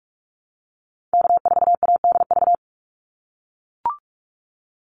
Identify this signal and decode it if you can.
Morse code, heard as a keyed tone, about 44 wpm, 700 Hz. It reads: K4ADV
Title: NA SOTA Activator Callsigns 45wpm